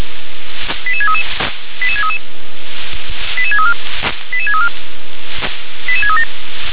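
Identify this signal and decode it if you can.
Five-tone selective call (ZVEI), heard as quick runs of five tones, tones of about 70 ms. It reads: 88633 88633 88634 88634 88637
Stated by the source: ZVEI five-tone selective call received on SDR.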